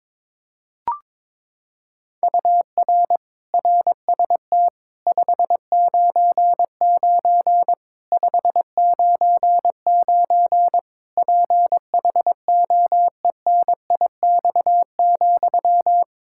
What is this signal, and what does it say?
Morse code, heard as a keyed tone, about 22 wpm, 700 Hz. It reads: UR RST 599 599 PHOENIX,
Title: K1USN SST QSO Practice 22wpm